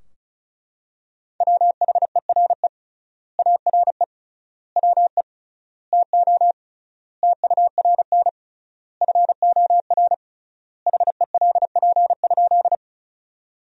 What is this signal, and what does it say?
Morse code, heard as a keyed tone, about 36 wpm, 700 Hz. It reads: WHERE ARE WE TO TURN FOR HELP?
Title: 3x Word-Spacing - Sentences from Top 200 Words - 35wpm